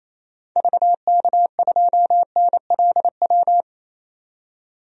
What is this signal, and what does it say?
Morse code, heard as a keyed tone, about 28 wpm, 700 Hz. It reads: VK2NLW
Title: DXCC-100 Call Signs - 6 Characters - 28wpm